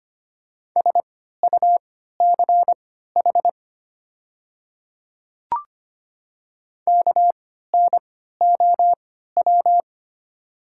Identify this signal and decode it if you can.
Morse code, heard as a keyed tone, about 25 wpm, spaced as for 14 wpm, 700 Hz. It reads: SUCH KNOW
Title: 3x Character Spacing - Top 200 Words - 25wpm